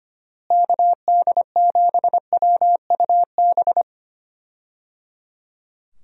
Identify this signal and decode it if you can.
Morse code, heard as a keyed tone, about 25 wpm, 700 Hz. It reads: KD7WUB
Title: USA Call Signs - 6 Characters - Repeated 3x - 25wpm